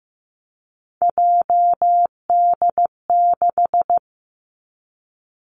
Morse code, heard as a keyed tone, about 15 wpm, 700 Hz. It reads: JD6
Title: Three Challenging Letter-Number Combinations 15wpm